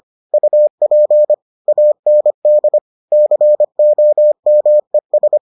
Morse code, heard as a keyed tone, about 25 wpm, 600 Hz. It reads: UP AND COMES